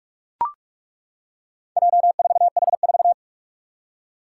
Morse code, heard as a keyed tone, about 41 wpm, 700 Hz. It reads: J4H4
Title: Four Challenging Letter-Number Combinations 45wpm